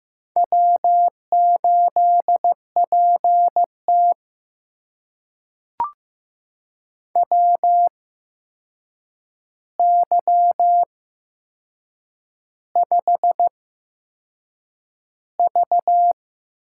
Morse code, heard as a keyed tone, about 15 wpm, spaced as for 6 wpm, 700 Hz. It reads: W8PT WY5V
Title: USA Call Signs - 4 Characters - 8x Character Spacing - 15wpm